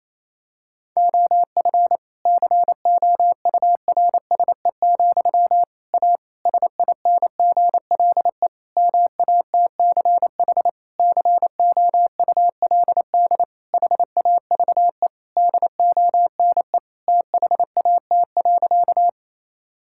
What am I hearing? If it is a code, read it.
Morse code, heard as a keyed tone, about 28 wpm, 700 Hz. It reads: OF COURSE, A SINGLE MATCH COULD HAVE DONE THAT.
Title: Sentences from Top 1600 Words 28wpm